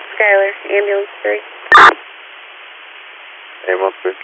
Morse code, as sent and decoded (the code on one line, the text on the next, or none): none